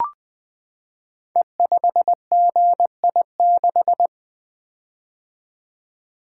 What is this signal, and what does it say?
Morse code, heard as a keyed tone, about 20 wpm, 700 Hz. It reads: E5GI6
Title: Code Groups - 5 Letter:Numbers - 20wpm